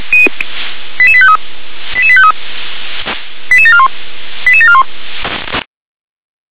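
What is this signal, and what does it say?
Five-tone selective call (ZVEI), heard as quick runs of five tones, tones of about 70 ms. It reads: 88853 88853 88741 88741
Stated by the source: ZVEI five-tone call 88835 received on an SDR.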